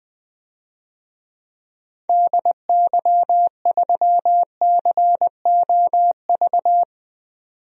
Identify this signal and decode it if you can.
Morse code, heard as a keyed tone, about 20 wpm, 700 Hz. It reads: DY3COV
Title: DXCC-100 Call Signs - 6 Characters - 20wpm